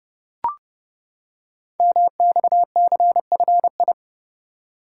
Morse code, heard as a keyed tone, about 30 wpm, 700 Hz. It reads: MXCFI
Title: Code Groups - 5 Letters - 30wpm